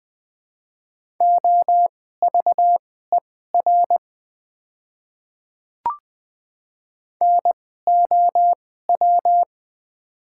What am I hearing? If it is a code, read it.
Morse code, heard as a keyed tone, about 21 wpm, spaced as for 15 wpm, 690 Hz. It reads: OVER NOW
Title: 2x Character Spacing - Top 100 Words - 20wpm